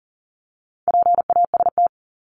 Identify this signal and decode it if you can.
Morse code, heard as a keyed tone, about 40 wpm, 700 Hz. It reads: PAST
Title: Top 501-750 Words 40wpm